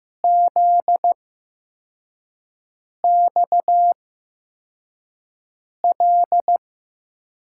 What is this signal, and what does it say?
Morse code, heard as a keyed tone, about 15 wpm, spaced as for 4 wpm, 700 Hz. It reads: ZXL
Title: USA Call Signs - 6 Characters - 8x Character Spacing - 15wpm